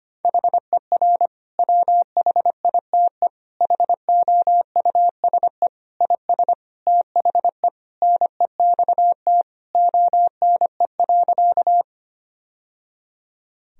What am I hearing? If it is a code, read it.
Morse code, heard as a keyed tone, about 25 wpm, 700 Hz. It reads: HER WHITE HOUSE IS THE NEXT ONE.
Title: Sentences from Top 1000 Words - Review Part 1 - 25wpm